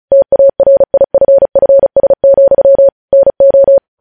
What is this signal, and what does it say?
Morse code, heard as a keyed tone, about 36 wpm, 550 Hz. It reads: TARIFFS, NO
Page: News Headlines in Morse Code at 35 WPM